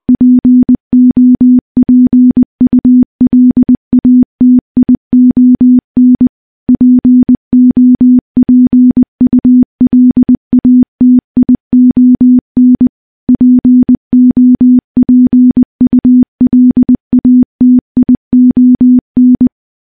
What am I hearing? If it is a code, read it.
Morse code, heard as a keyed tone, about 20 wpm, 250 Hz. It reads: POPULATION POPULATION POPULATION
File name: morse-training-20WPM